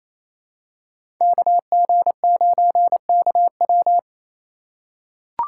Morse code, heard as a keyed tone, about 28 wpm, 700 Hz. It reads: KG9KW
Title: USA Call Signs - 5 Characters - 2x Character Spacing - 28wpm